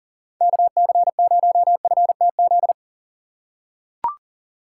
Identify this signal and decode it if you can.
Morse code, heard as a keyed tone, about 40 wpm, 700 Hz. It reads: KC0FTZ
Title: USA Call Signs - 6 Characters - 4x Character Spacing - 40wpm